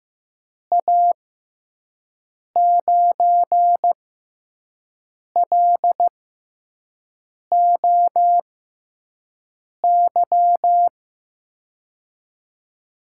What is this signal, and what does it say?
Morse code, heard as a keyed tone, about 15 wpm, spaced as for 5 wpm, 700 Hz. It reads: A9LOY
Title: USA Call Signs - 6 Characters - 6x Character Spacing - 15wpm